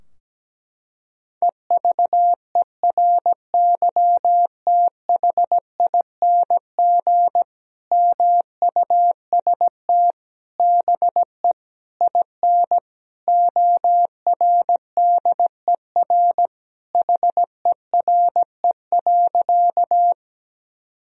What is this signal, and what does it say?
Morse code, heard as a keyed tone, about 17 wpm, 700 Hz. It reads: EVERYTHING MUST BE IN ORDER HERE.
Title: Sentences from Top 1000 Words - Review Part 2 - 17wpm